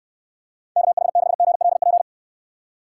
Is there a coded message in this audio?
Yes: Morse code from 0.8 to 2.0 s, heard as a keyed tone, about 67 wpm, 700 Hz. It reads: KH6GLP